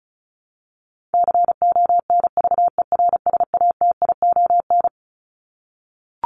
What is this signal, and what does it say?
Morse code, heard as a keyed tone, about 34 wpm, 700 Hz. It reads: CONVERSATION
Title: Top 1751-2000 Words 35wpm